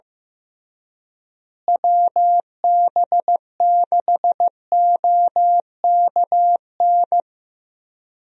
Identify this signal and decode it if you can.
Morse code, heard as a keyed tone, about 15 wpm, 700 Hz. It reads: WB6OKN
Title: USA Call Signs - 6 Characters - Repeated 3x - 15wpm